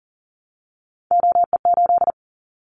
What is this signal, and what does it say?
Morse code, heard as a keyed tone, about 40 wpm, 700 Hz. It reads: OE8
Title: Callsign Prefixes for Top 100 DXCC - 40wpm